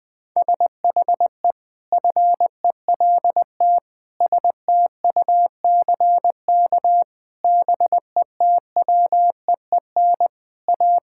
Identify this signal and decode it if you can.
Morse code, heard as a keyed tone, about 21 wpm, 700 Hz. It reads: SHE FELT STUCK BETWEEN A
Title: Sentences with 8-Word Phrases - 20wpm